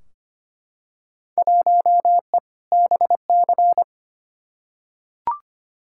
Morse code, heard as a keyed tone, about 25 wpm, 700 Hz. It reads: 1E BC
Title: Field Day Exchanges - 25wpm